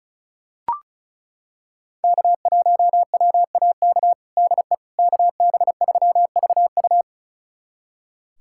Morse code, heard as a keyed tone, about 36 wpm, 700 Hz. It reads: K1WAK DE KB3VU
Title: USA Callsign DE Callsign - 5 Characters - 35wpm